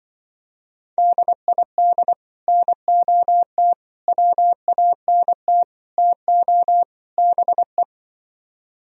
Morse code, heard as a keyed tone, about 24 wpm, 700 Hz. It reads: DID NOT WANT TO BE
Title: Speed-Racing - Sets of 5 Words 20wpm